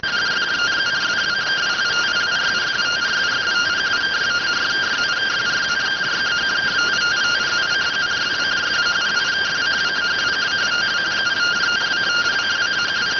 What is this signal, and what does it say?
Two-tone teleprinter signal, about 170 Hz shift.